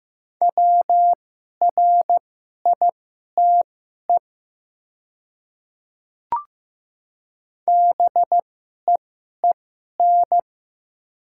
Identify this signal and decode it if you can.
Morse code, heard as a keyed tone, about 15 wpm, spaced as for 11 wpm, 700 Hz. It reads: WRITE BEEN
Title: 2x Character Spacing - Top 100 Words - 15wpm